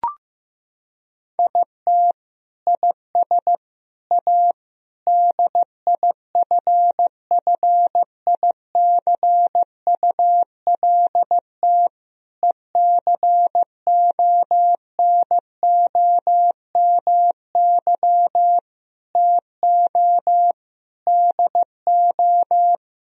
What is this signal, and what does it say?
Morse code, heard as a keyed tone, about 16 wpm, 700 Hz. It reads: IT IS A DIFFICULT ECONOMY TO DO
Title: Sentences from Top 1000 Words - Review Part 3 - 15wpm